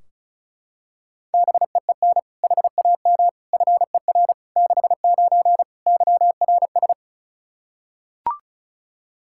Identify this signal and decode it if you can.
Morse code, heard as a keyed tone, about 34 wpm, 700 Hz. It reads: BEEN HAM FER 69 YRS
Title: Practice QSO - Exchange 3 - More Info - 35wpm